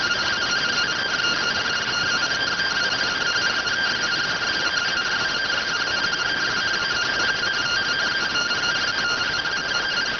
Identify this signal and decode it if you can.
Two-tone teleprinter signal, about 170 Hz shift.